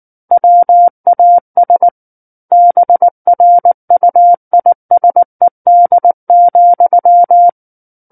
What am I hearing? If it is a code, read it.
Morse code, heard as a keyed tone, about 19 wpm, 700 Hz. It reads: WAS BRUISED,
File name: Book One / 10-Chapter 10 - 19 WPM